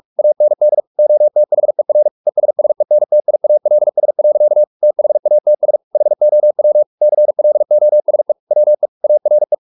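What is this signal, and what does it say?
Morse code, heard as a keyed tone, about 44 wpm, 600 Hz. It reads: AND OTHER ESSENTIALS. THATS HOW CLOSE WE ARE
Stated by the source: Morse practice at 45 wpm, the novel Never by Ken Follett in English, from CW med Gnister.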